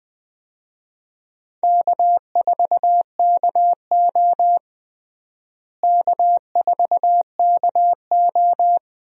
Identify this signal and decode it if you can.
Morse code, heard as a keyed tone, about 20 wpm, 700 Hz. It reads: K4KO K4KO
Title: USA Call Signs - 4 Characters - Repeated 2x - 20wpm